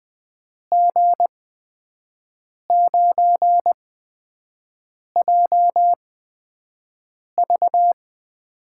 Morse code, heard as a keyed tone, about 20 wpm, spaced as for 5 wpm, 700 Hz. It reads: G9JV
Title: USA Call Signs - 8x Character Spacing - 20wpm